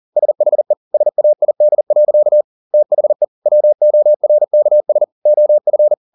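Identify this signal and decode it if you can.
Morse code, heard as a keyed tone, about 40 wpm, 600 Hz. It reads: SHE SAID. THE WORKS OF
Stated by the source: Morse code at 40 wpm, practice text, CW med Gnister.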